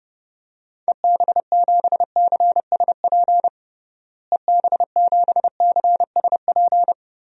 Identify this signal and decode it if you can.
Morse code, heard as a keyed tone, about 32 wpm, 700 Hz. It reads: EB7CSP EB7CSP
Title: DXCC-100 Call Signs - 6 Characters - Repeated 3x - 30wpm